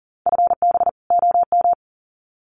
Morse code, heard as a keyed tone, about 40 wpm, 700 Hz. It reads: FB OM
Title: Common QSO Phrases 40wpm